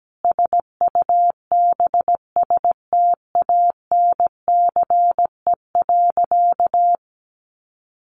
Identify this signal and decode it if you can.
Morse code, heard as a keyed tone, about 17 wpm, 700 Hz. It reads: SUBSTANCE.
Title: Sentences from Top 1500 Words 17wpm